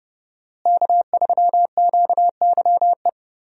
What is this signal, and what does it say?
Morse code, heard as a keyed tone, about 30 wpm, 700 Hz. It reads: K3QYE
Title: USA Call Signs - 5 Characters - Repeated 3x - 30wpm